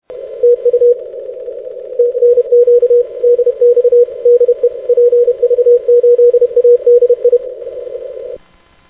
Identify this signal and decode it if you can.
Morse code, heard as a keyed tone, about 31 wpm, 470 Hz. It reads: TU TNQ DX DE PV8ADI
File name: 6m-pv8adi